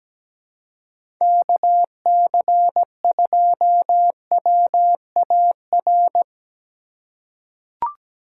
Morse code, heard as a keyed tone, about 17 wpm, 700 Hz. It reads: KC2WAR
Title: USA Call Signs - 6 Characters - 3x Character Spacing - 17wpm